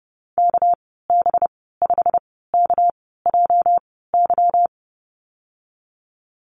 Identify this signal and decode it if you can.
Morse code, heard as a keyed tone, about 32 wpm, spaced as for 17 wpm, 700 Hz. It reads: KB5KJY
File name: Lesson 37g - Letter x - Call Signs 30wpm